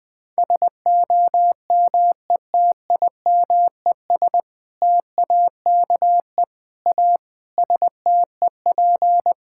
Morse code, heard as a keyed tone, about 19 wpm, 700 Hz. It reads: SOMETIMES TAKE A STEP
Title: Sentences with 6-Word Phrases - Spoken Beginning - 20wpm